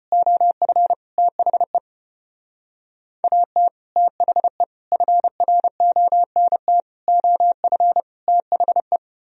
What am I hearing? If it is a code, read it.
Morse code, heard as a keyed tone, about 30 wpm, 700 Hz. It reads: OF THE AT THE FRONT OF THE
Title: Speed-Racing - Sets of 5 Words 25wpm